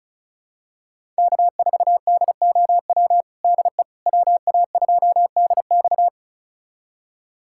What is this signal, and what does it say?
Morse code, heard as a keyed tone, about 36 wpm, 700 Hz. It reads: K4DOW DE WA2DX
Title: USA Callsign DE Callsign - 5 Characters - 35wpm